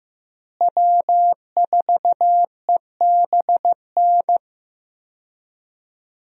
Morse code, heard as a keyed tone, about 15 wpm, 700 Hz. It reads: W4EBN
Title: USA Call Signs - 5 Characters - Repeated 2x - 15wpm